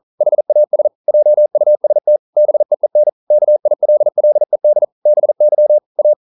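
Morse code, heard as a keyed tone, about 41 wpm, 600 Hz. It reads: HAS JUST BEEN KILLED BY A
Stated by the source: Morse code practice at 41 wpm, from CW med Gnister.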